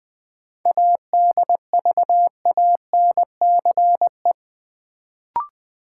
Morse code, heard as a keyed tone, about 20 wpm, 700 Hz. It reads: ADVANCE
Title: Sentences from Top 1400 Words - New Word Drill - 20wpm